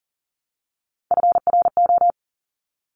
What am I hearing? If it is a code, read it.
Morse code, heard as a keyed tone, about 41 wpm, 700 Hz. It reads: FRO
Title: Top Three Letter Combinations in Words 40wpm